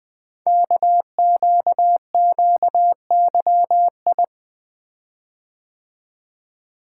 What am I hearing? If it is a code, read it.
Morse code, heard as a keyed tone, about 20 wpm, 700 Hz. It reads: KQQYI
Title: Code Groups - 5 Letter:Numbers - 20wpm